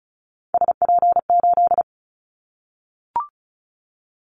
Morse code, heard as a keyed tone, about 34 wpm, 700 Hz. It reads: SP8
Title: Callsign Prefixes for Top 25 DXCC - Repeated 3x - 35wpm